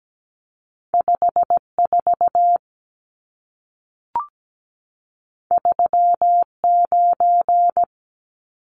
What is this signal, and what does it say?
Morse code, heard as a keyed tone, about 17 wpm, 700 Hz. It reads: 54 39
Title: Numbers - 2-Digit 17wpm